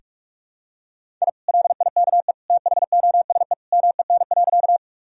Morse code, heard as a keyed tone, about 46 wpm, 700 Hz. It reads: I LIKE THOSE MEN.
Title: Sentences from Top 1000 Words - Review Part 1 - 45wpm